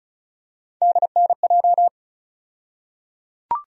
Morse code, heard as a keyed tone, about 36 wpm, 700 Hz. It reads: DNJ
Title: USA Call Sign Postfixes - 3 Letters 35wpm